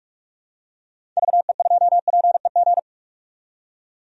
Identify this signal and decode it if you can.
Morse code, heard as a keyed tone, about 42 wpm, 700 Hz. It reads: VE2PEG
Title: DXCC-100 Call Signs - 6 Characters - 45wpm